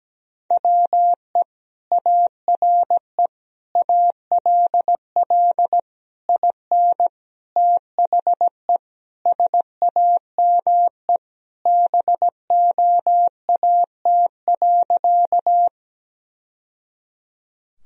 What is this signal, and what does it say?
Morse code, heard as a keyed tone, about 17 wpm, 700 Hz. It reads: WE ARE ALL IN THE SAME BOAT.
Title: Sentences from Top 1000 Words - Review Part 3 - 17wpm